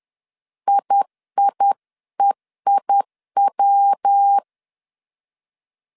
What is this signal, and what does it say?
Morse code, heard as a keyed tone, about 10 wpm, 800 Hz. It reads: IIEIW
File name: CW / DET / 1 3 1500C S C 10wpm 3335